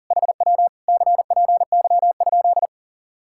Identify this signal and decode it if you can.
Morse code, heard as a keyed tone, about 50 wpm, 700 Hz. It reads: HW CPY?